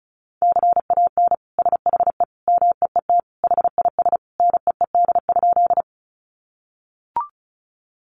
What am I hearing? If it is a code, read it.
Morse code, heard as a keyed tone, about 34 wpm, 700 Hz. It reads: CAN SHE MEET HIS NEED?
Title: Sentences from Top 1100 Words 35wpm